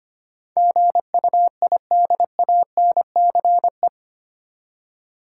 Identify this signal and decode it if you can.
Morse code, heard as a keyed tone, about 25 wpm, 700 Hz. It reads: GUIDANCE
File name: Eight Letter Words - 25wpm